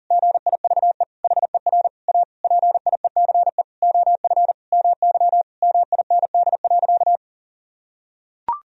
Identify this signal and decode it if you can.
Morse code, heard as a keyed tone, about 40 wpm, 700 Hz. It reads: GIVE HER A PIECE OF MY MIND.
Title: Sentences from Top 1000 Words - Review Part 2 - 40wpm